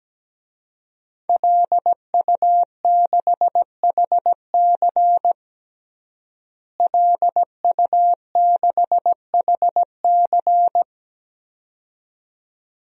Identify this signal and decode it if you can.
Morse code, heard as a keyed tone, about 17 wpm, 700 Hz. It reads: LU6HC LU6HC
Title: DXCC-100 Call Signs - 5 Characters - Repeated 2x - 17wpm